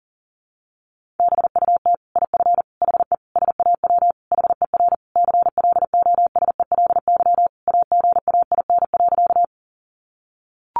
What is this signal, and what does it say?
Morse code, heard as a keyed tone, about 40 wpm, 700 Hz. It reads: BUT IF HE SAW HER CLOSELY AGAIN.
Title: Sentences from Top 1400 Words 40wpm